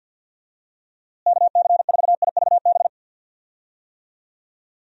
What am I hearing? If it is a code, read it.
Morse code, heard as a keyed tone, about 48 wpm, 700 Hz. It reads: KC4IVB
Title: USA Call Signs - 6 Characters - Repeated 3x - 50wpm